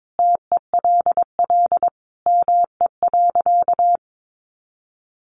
Morse code, heard as a keyed tone, about 22 wpm, 700 Hz. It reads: TELL ME.
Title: Sentences from Top 200 Words 22wpm